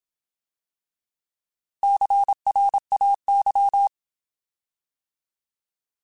Morse code, heard as a keyed tone, about 27 wpm, 800 Hz. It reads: CRAY
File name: Alphabetical / Alphabetical mp3s / Cs @25